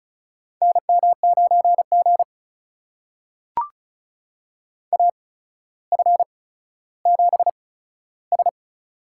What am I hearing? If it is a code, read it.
Morse code, heard as a keyed tone, about 36 wpm, spaced as for 15 wpm, 700 Hz. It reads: NM9G AF7S
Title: USA Call Signs - 4 Characters - 8x Character Spacing - 35wpm